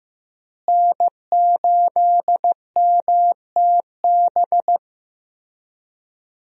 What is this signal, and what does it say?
Morse code, heard as a keyed tone, about 16 wpm, 700 Hz. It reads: N8MTB
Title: USA Call Signs - 5 Characters - Repeated 3x - 15wpm